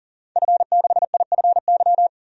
Morse code, heard as a keyed tone, about 40 wpm, 700 Hz. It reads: F6IFY
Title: DXCC-100 Call Signs - 5 Characters - 40wpm